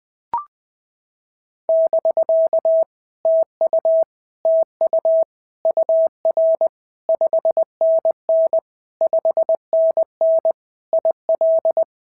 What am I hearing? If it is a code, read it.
Morse code, heard as a keyed tone, about 20 wpm, 650 Hz. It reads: <BK> TU TU UR 5NN 5NN IL
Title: POTA QSO Practice - 20wpm